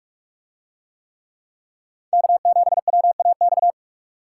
Morse code, heard as a keyed tone, about 46 wpm, 700 Hz. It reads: K7WAX